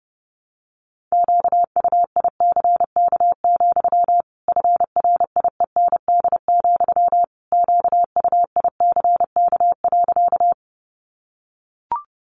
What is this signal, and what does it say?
Morse code, heard as a keyed tone, about 30 wpm, 700 Hz. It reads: QUICK, FRIEND, QUICK.
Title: Sentences from Top 500 Words 30wpm